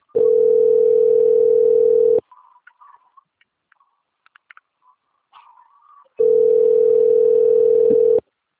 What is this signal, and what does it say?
Ringback tone (North America), 440 + 480 Hz.
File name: out- - - - - .4852